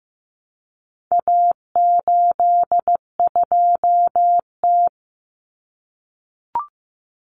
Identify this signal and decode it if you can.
Morse code, heard as a keyed tone, about 15 wpm, 700 Hz. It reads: A82T
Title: Common Cut Numbers - 4-Digit 15wpm